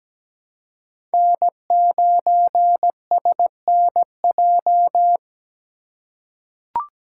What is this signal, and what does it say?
Morse code, heard as a keyed tone, about 17 wpm, 700 Hz. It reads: N9SNJ
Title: USA Call Signs - 5 Characters - 4x Character Spacing - 17wpm